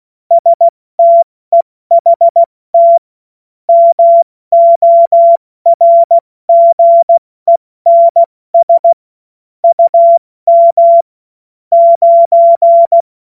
Morse code, heard as a keyed tone, about 16 wpm, 680 Hz. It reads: STEHT MORGENS UM 9
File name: Morsekurs / Witze / Witz 23. Dezember 2025 70 F80